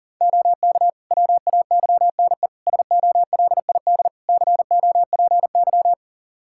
Morse code, heard as a keyed tone, about 37 wpm, 700 Hz. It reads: OK WAYNE SOLID COPY
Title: Practice QSO - Exchange 3 - More Info - 40wpm